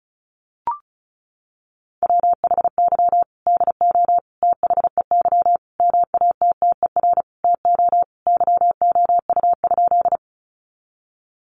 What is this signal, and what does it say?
Morse code, heard as a keyed tone, about 34 wpm, 700 Hz. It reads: WHY DO THEY MATTER TO YOU?
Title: Sentences from Top 700 Words 35wpm